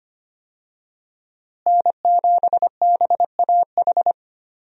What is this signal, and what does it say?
Morse code, heard as a keyed tone, about 25 wpm, 700 Hz. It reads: N7BAH
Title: USA Call Signs - 5 Characters - Repeated 2x - 25wpm